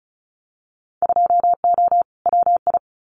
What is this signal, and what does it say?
Morse code, heard as a keyed tone, about 34 wpm, 700 Hz. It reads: 2O WI
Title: Winter Field Day Exchanges 35wpm